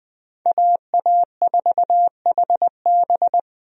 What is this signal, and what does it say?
Morse code, heard as a keyed tone, about 20 wpm, 700 Hz. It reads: AA4HB
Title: USA Call Signs - 5 Characters - Repeated 2x - 20wpm